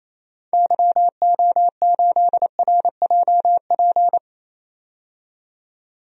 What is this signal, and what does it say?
Morse code, heard as a keyed tone, about 28 wpm, 700 Hz. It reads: YO8RJP
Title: DXCC-100 Call Signs - 6 Characters - 28wpm